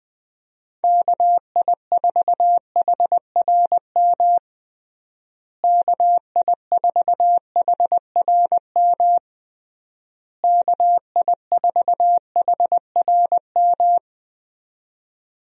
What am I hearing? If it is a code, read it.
Morse code, heard as a keyed tone, about 20 wpm, 700 Hz. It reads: KI4HRM KI4HRM KI4HRM
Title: USA Call Signs - 6 Characters - Repeated 3x - 20wpm